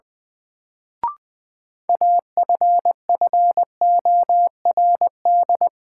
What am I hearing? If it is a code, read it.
Morse code, heard as a keyed tone, about 20 wpm, 700 Hz. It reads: AFFORD
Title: Sentences from Top 1100 Words - New Word Drill Alt - 20wpm